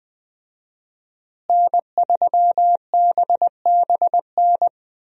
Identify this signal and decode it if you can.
Morse code, heard as a keyed tone, about 20 wpm, 700 Hz. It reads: N3BBN
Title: USA Call Signs - 5 Characters - Repeated 2x - 20wpm